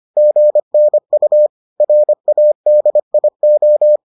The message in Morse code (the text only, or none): GNU RADIO